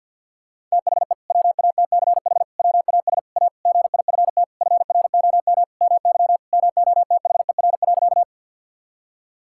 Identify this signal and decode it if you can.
Morse code, heard as a keyed tone, about 50 wpm, 700 Hz. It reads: THE WATCH WAS A GIFT FROM MY MOTHER.
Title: Sentences from Top 1000 Words - Review Part 6 - 50wpm